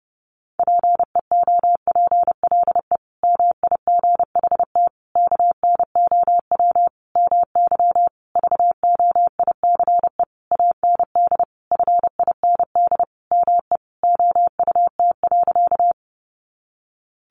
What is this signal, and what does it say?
Morse code, heard as a keyed tone, about 30 wpm, 700 Hz. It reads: PEOPLE MIGHT KNOW MY VOICE AND FIND ME OUT.
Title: Sentences from Top 500 Words 30wpm